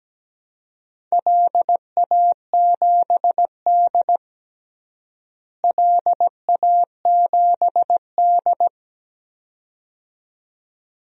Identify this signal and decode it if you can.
Morse code, heard as a keyed tone, about 17 wpm, 700 Hz. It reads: LA7D LA7D
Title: DXCC-100 Call Signs - 4 Characters - Repeated 2x - 17wpm